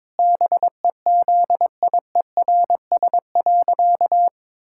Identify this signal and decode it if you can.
Morse code, heard as a keyed tone, about 22 wpm, 700 Hz. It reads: BEZIERS.
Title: Pangram Challenge v2 - 22wpm